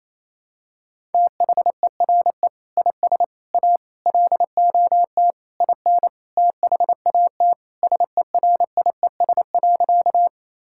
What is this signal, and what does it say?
Morse code, heard as a keyed tone, about 28 wpm, 700 Hz. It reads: THERE IS A LOT IN THAT SERIES.
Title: Sentences from Top 1000 Words - Review Part 5 - 28wpm